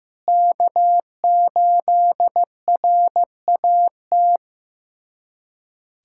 Morse code, heard as a keyed tone, about 16 wpm, 700 Hz. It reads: K8RAT
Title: USA Callsign DE Callsign 15wpm